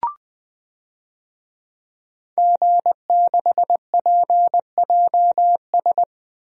Morse code, heard as a keyed tone, about 20 wpm, 700 Hz. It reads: G6PJS
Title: DXCC-100 Call Signs - 5 Characters - Repeated 2x - 20wpm